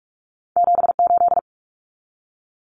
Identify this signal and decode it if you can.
Morse code, heard as a keyed tone, about 46 wpm, 700 Hz. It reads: 78
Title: Common Cut Numbers - 2-Digit 45wpm